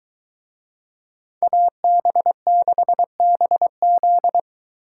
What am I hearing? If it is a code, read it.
Morse code, heard as a keyed tone, about 22 wpm, 700 Hz. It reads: AB6BZ